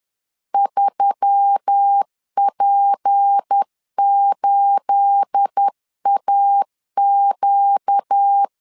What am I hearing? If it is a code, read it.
Morse code, heard as a keyed tone, about 10 wpm, 800 Hz. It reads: 3P8AQ
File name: LEZIONI / 9 28 10wpm 2842